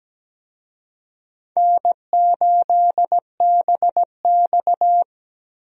Morse code, heard as a keyed tone, about 17 wpm, 700 Hz. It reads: N8BX
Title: USA Call Signs - 4 Characters - Repeated 3x - 17wpm